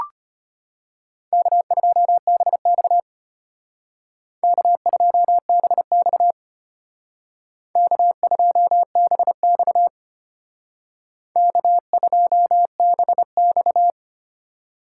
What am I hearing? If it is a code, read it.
Morse code, heard as a keyed tone, about 30 wpm, 700 Hz. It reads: K2BX K2BX K2BX K2BX
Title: Speed Racing - USA 1x2 Call Signs 25wpm